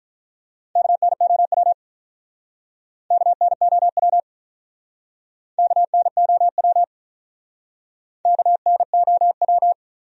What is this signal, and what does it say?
Morse code, heard as a keyed tone, about 45 wpm, 700 Hz. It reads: KNOW KNOW KNOW KNOW